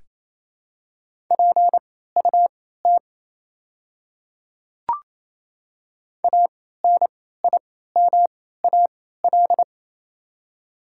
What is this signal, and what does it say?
Morse code, heard as a keyed tone, about 28 wpm, spaced as for 16 wpm, 700 Hz. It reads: PUT ANIMAL
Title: 3x Character Spacing - Top 200 Words - 28wpm